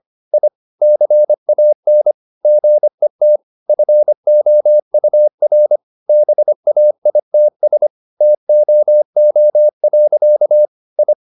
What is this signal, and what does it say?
Morse code, heard as a keyed tone, about 25 wpm, 600 Hz. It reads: I CAN GET FOUR BAITS TOO. I
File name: the old man and the sea part 1 25wpm